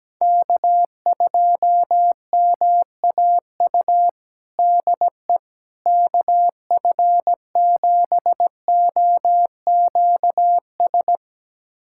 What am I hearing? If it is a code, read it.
Morse code, heard as a keyed tone, about 17 wpm, 700 Hz. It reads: K2MAU DE KF7OQS
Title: USA Callsign DE Callsign 17wpm